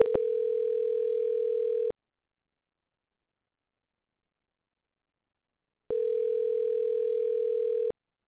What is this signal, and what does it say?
Ringback tone (North America), 440 + 480 Hz.